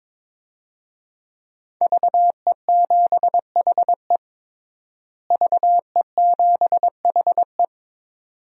Morse code, heard as a keyed tone, about 22 wpm, 700 Hz. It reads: VE7HE VE7HE
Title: DXCC-100 Call Signs - 5 Characters - Repeated 2x - 22wpm